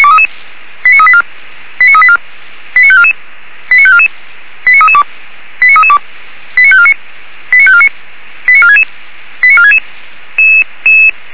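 Five-tone selective call (ZVEI), heard as quick runs of five tones, tones of about 70 ms. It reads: 89384 89384 89544 89544 89332 89332 89549 89549 89466 89466
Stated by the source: Five-tone ZVEI selective call received on an SDR.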